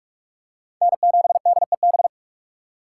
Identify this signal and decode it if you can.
Morse code, heard as a keyed tone, about 47 wpm, 690 Hz. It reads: N7DEB
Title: USA Call Signs - 5 Characters - 2x Character Spacing - 45wpm